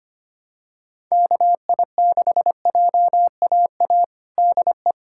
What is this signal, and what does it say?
Morse code, heard as a keyed tone, about 25 wpm, 700 Hz. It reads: KI6JAA DE
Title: USA Callsign DE Callsign 25wpm